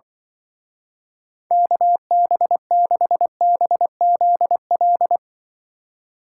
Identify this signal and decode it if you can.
Morse code, heard as a keyed tone, about 24 wpm, 700 Hz. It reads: KB6BZL
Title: Speed Racing - USA Call Signs 20wpm